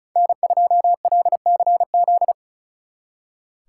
Morse code, heard as a keyed tone, about 43 wpm, 700 Hz. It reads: N2LCZ
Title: Speed Racing - USA Call Signs 35wpm